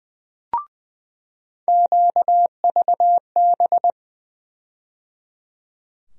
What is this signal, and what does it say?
Morse code, heard as a keyed tone, about 20 wpm, 700 Hz. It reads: QVB
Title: USA Call Sign Postfixes - 3 Letters 20wpm